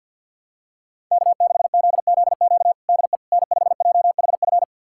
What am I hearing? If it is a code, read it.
Morse code, heard as a keyed tone, about 52 wpm, 700 Hz. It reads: K6ZZQ DE N5JHF
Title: USA Callsign DE Callsign - 5 Characters - 50wpm